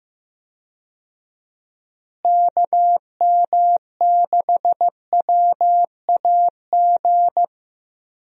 Morse code, heard as a keyed tone, about 15 wpm, 700 Hz. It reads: KM6WAG
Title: USA Call Signs - 6 Characters - Repeated 3x - 15wpm